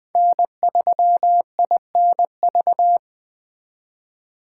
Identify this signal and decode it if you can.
Morse code, heard as a keyed tone, about 20 wpm, 700 Hz. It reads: N3INV